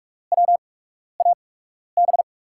Morse code, heard as a keyed tone, about 44 wpm, spaced as for 12 wpm, 700 Hz. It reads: WAB
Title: USA Call Signs - 5 Characters - 8x Character Spacing - 45wpm